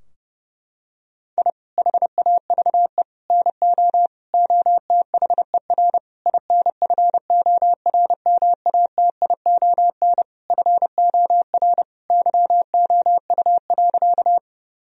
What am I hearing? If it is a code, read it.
Morse code, heard as a keyed tone, about 30 wpm, 700 Hz. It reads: I HAVE NO OTHER INFORMATION FOR YOU.
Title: Sentences from Top 1000 Words - Review Part 3 - 30wpm